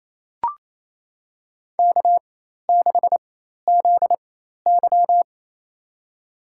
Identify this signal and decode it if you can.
Morse code, heard as a keyed tone, about 28 wpm, spaced as for 13 wpm, 700 Hz. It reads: K6ZY
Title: USA Call Signs - 4 Characters - 4x Character Spacing - 28wpm